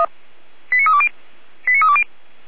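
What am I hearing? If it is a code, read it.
Five-tone selective call (ZVEI), heard as quick runs of five tones, tones of about 70 ms. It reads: 89320 89320
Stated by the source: ZVEI five-tone selective call received by SDR.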